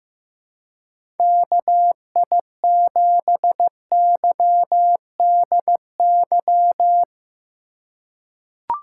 Morse code, heard as a keyed tone, about 15 wpm, 700 Hz. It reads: KI7YDY
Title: USA Call Signs - 6 Characters - 2x Character Spacing - 15wpm